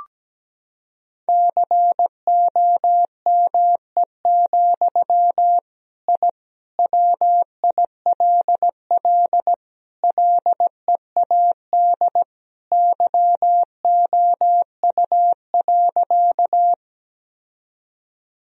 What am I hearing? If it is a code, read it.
Morse code, heard as a keyed tone, about 17 wpm, 700 Hz. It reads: COME, I WILL LEAD YOU.
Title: Sentences from Top 1000 Words - Review Part 2 - 17wpm